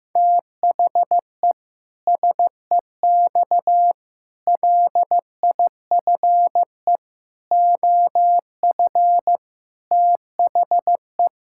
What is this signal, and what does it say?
Morse code, heard as a keyed tone, about 15 wpm, 700 Hz. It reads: THE SEX LIFE OF THE
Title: Pangram Challenge v2 - 15wpm